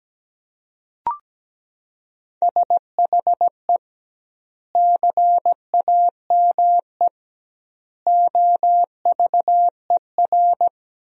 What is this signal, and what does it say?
Morse code, heard as a keyed tone, about 17 wpm, 700 Hz. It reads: SHE CAME OVER
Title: 2x Word-Spacing - Sentences from Top 300 Words - 17wpm